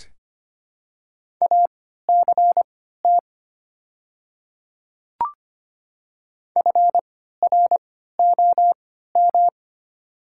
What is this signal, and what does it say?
Morse code, heard as a keyed tone, about 25 wpm, spaced as for 14 wpm, 690 Hz. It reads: ACT FROM
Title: 3x Character Spacing - Top 200 Words - 25wpm